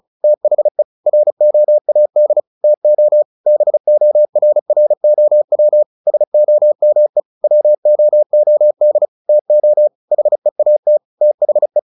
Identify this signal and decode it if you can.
Morse code, heard as a keyed tone, about 35 wpm, 600 Hz. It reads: THE ROAD TO BORROW SOME WOOD TO HEAT THE
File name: the old man and the sea 2 35wpm